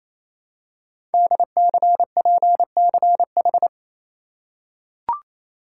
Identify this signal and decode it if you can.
Morse code, heard as a keyed tone, about 28 wpm, 700 Hz. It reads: DCPCH
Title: Code Groups - 5 Letters - 28wpm